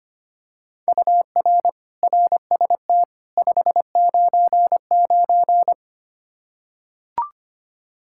Morse code, heard as a keyed tone, about 25 wpm, 700 Hz. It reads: UR RST 599
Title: Practice QSO - Exchange 1 - Beginning - 25wpm